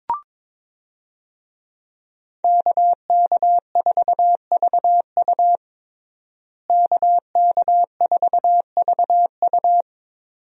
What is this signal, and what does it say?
Morse code, heard as a keyed tone, about 22 wpm, 700 Hz. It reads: KK4VU KK4VU
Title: USA Call Signs - 5 Characters - Repeated 2x - 22wpm